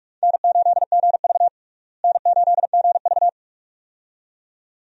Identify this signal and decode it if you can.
Morse code, heard as a keyed tone, about 46 wpm, 700 Hz. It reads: N8GV N8GV
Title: USA Call Signs - 4 Characters - Repeated 2x - 45wpm